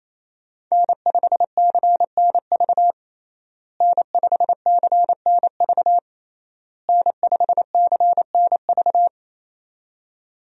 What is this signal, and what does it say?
Morse code, heard as a keyed tone, about 28 wpm, 700 Hz. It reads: N5CNV N5CNV N5CNV
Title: USA Call Signs - 5 Characters - Repeated 3x - 28wpm